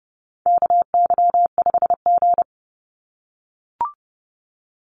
Morse code, heard as a keyed tone, about 30 wpm, 700 Hz. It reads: KY5G
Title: USA Call Signs 30wpm